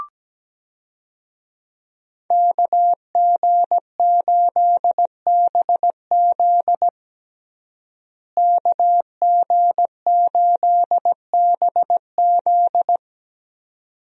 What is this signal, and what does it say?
Morse code, heard as a keyed tone, about 17 wpm, 700 Hz. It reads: KG8BZ KG8BZ
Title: USA Call Signs - 5 Characters - Repeated 2x - 17wpm